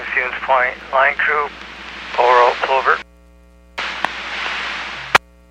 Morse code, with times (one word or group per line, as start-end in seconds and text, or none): none